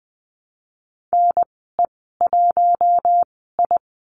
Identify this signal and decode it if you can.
Morse code, heard as a keyed tone, about 20 wpm, spaced as for 14 wpm, 700 Hz. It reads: NE1I
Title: 128 Lesson 21g - Number 6 - Call Signs 20wpm